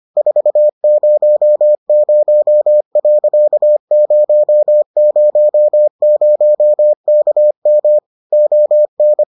Morse code, heard as a keyed tone, about 25 wpm, 600 Hz. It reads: 400.000KM ON